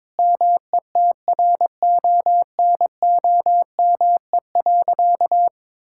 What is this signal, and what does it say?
Morse code, heard as a keyed tone, about 22 wpm, 700 Hz. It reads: METRONOME.
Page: morse-code-ninja-practice-22wpm